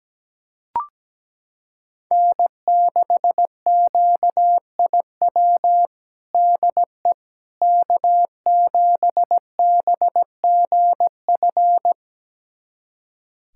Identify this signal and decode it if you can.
Morse code, heard as a keyed tone, about 17 wpm, 700 Hz. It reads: N6QIW DE K7BGF
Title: USA Callsign DE Callsign - 5 Characters - 17wpm